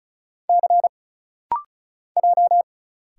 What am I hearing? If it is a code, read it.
Morse code, heard as a keyed tone, about 34 wpm, 700 Hz. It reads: C J
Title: Single Letters - Rapid-Fire 35wpm